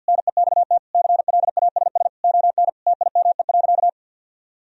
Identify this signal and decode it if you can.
Morse code, heard as a keyed tone, about 50 wpm, 700 Hz. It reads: NEXT CLASS ON TIME.
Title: Sentences from Top 1000 Words - Review Part 2 - 50wpm